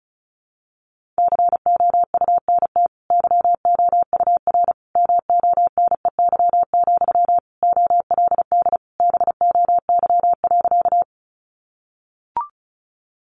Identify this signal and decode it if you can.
Morse code, heard as a keyed tone, about 34 wpm, 700 Hz. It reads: COUNT YOUR MONEY, OLD BOY.
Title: Sentences from Top 1500 Words 35wpm